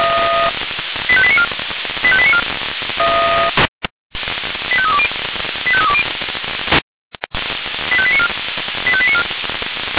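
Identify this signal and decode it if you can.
Five-tone selective call (ZVEI), heard as quick runs of five tones, tones of about 80 ms. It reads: 85894 85894 85320 85320 85894 85894
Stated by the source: ZVEI five-tone call received by SDR.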